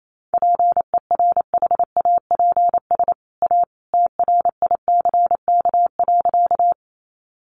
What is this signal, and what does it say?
Morse code, heard as a keyed tone, about 28 wpm, 700 Hz. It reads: PERHAPS A TRICK.
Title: Sentences from Top 1500 Words 28wpm